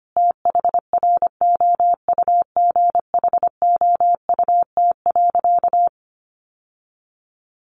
Morse code, heard as a keyed tone, about 25 wpm, 700 Hz. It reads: THROUGHOUT.
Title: Sentences from Top 800 Words 25wpm